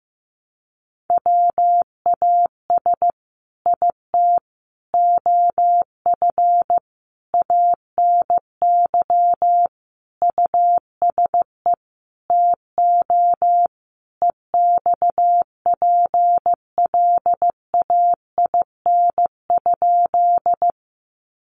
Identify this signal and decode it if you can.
Morse code, heard as a keyed tone, about 15 wpm, 700 Hz. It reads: WAS IT OF ANY USE TO EXPLAIN?
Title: Sentences from Top 1100 Words 15wpm